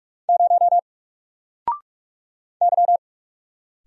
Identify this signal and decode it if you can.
Morse code, heard as a keyed tone, about 44 wpm, 700 Hz. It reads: 0 Y
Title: Single Letter-Number Alt 45wpm